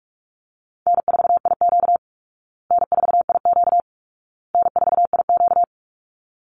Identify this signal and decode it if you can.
Morse code, heard as a keyed tone, about 44 wpm, 700 Hz. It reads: N4IQ N4IQ N4IQ
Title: Easy Call Signs Repeated 3x - 45wpm